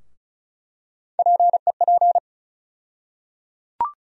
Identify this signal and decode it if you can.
Morse code, heard as a keyed tone, about 36 wpm, 700 Hz. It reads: PEP
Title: USA Call Sign Postfixes - 3 Letters 35wpm